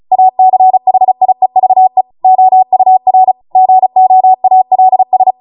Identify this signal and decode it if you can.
Morse code, heard as a keyed tone, about 36 wpm, 750 Hz. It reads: ACHIEVE OUR GOALS